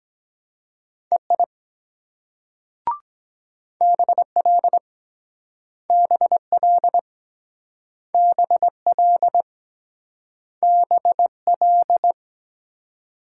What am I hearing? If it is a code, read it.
Morse code, heard as a keyed tone, about 23 wpm, 700 Hz. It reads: EI BL BL BL BL